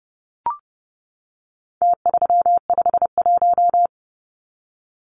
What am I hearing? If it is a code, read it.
Morse code, heard as a keyed tone, about 30 wpm, 700 Hz. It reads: T351
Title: Numbers - Contest Style - 4-Digit 30wpm